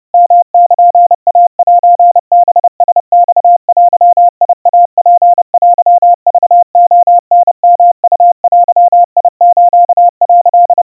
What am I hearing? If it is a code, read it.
Morse code (Japanese Wabun code), heard as a keyed tone, about 30 wpm, 700 Hz. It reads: ヨルイセハラマデイツテクレタヨウデス」
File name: 110 cpm 700 Hz Crispy